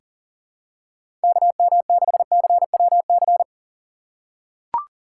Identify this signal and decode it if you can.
Morse code, heard as a keyed tone, about 40 wpm, 700 Hz. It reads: KM6CWC
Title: USA Call Signs - 6 Characters - 4x Character Spacing - 40wpm